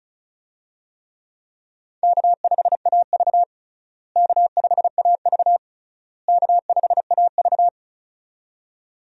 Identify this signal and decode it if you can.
Morse code, heard as a keyed tone, about 36 wpm, 700 Hz. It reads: K5AV K5AV K5AV
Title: USA Call Signs - 4 Characters - Repeated 3x - 35wpm